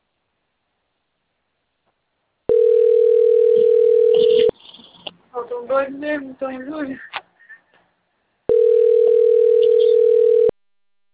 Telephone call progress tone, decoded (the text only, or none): ringback tone